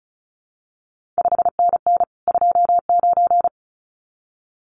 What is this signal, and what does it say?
Morse code, heard as a keyed tone, about 36 wpm, 700 Hz. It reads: 5NN 29
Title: CQ WW DX Contest Exchange - 35wpm